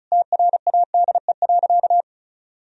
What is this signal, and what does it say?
Morse code, heard as a keyed tone, about 36 wpm, 700 Hz. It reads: TRADE.
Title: Sentences from Top 1000 Words - Review Part 4 - 35wpm